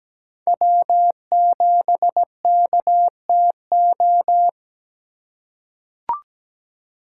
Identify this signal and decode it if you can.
Morse code, heard as a keyed tone, about 17 wpm, 700 Hz. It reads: W7KTO
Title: USA Call Signs - 5 Characters - 3x Character Spacing - 17wpm